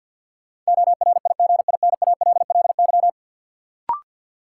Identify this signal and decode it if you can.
Morse code, heard as a keyed tone, about 49 wpm, 700 Hz. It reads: ORIGINALLY
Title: Sentences from Top 1500 Words - New Word Drill - 50wpm